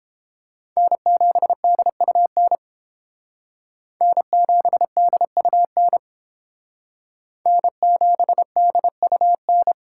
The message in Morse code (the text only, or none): N7DUN N7DUN N7DUN